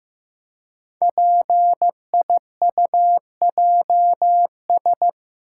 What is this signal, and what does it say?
Morse code, heard as a keyed tone, about 15 wpm, 700 Hz. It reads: PIUJS
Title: Code Groups - 5 Letters - 15wpm